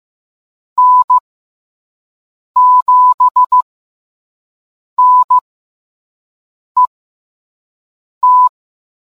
Morse code, heard as a keyed tone, about 15 wpm, spaced as for 5 wpm, 1000 Hz. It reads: N7NET